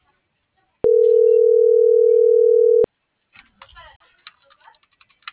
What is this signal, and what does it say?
Ringback tone (North America), 440 + 480 Hz.